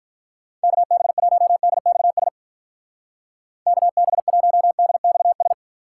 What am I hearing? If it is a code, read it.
Morse code, heard as a keyed tone, about 43 wpm, 700 Hz. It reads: KB1DXS KB1DXS
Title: Speed Racing - USA 2x3 Call Signs 35wpm